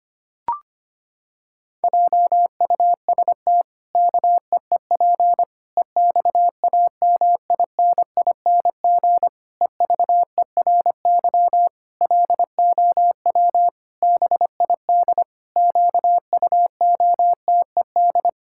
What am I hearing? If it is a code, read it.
Morse code, heard as a keyed tone, about 25 wpm, 700 Hz. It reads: JUST KEEP EXAMINING EVERY LOW BID QUOTED
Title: Pangram Challenge v2 - 25wpm